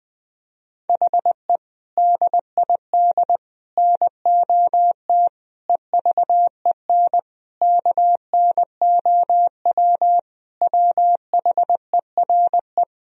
Morse code, heard as a keyed tone, about 20 wpm, 700 Hz. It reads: HE DID NOT EVEN KNOW WHERE
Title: Sentences from Top 1000 Words - Review Part 1 - 20wpm